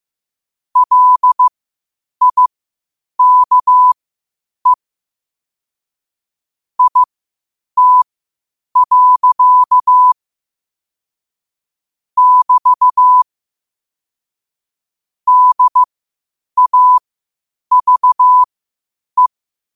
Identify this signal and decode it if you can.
Morse code, heard as a keyed tone, about 15 wpm, spaced as for 8 wpm, 1000 Hz. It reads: LIKE IT. <BT> DAVE
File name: qotdfiles / 8wpm